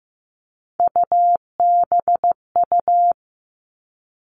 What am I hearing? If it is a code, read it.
Morse code, heard as a keyed tone, about 15 wpm, 700 Hz. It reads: UBU
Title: Three Challenging Letter-Number Combinations 15wpm